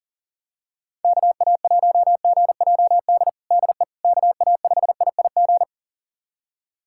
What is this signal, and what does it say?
Morse code, heard as a keyed tone, about 40 wpm, 700 Hz. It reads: KA1GJD DE KA5IIG
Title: USA Callsign DE Callsign - 6 Characters - 40wpm